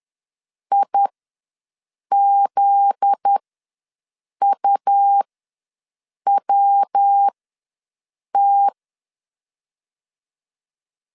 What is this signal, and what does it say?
Morse code, heard as a keyed tone, about 10 wpm, spaced as for 6 wpm, 800 Hz. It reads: IZUWT
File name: LEZIONI / 7 GF9Z / 7 GF9Z ws 10wpm 3919